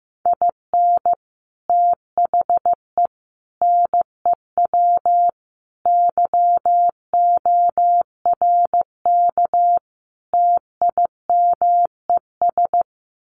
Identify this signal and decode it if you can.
Morse code, heard as a keyed tone, about 15 wpm, 700 Hz. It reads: IN THE NEW YORK TIMES
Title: Sets of 5 Words 15wpm